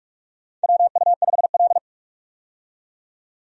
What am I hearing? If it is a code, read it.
Morse code, heard as a keyed tone, about 44 wpm, 700 Hz. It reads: WU5L
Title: USA Call Signs - 4 Characters - 45wpm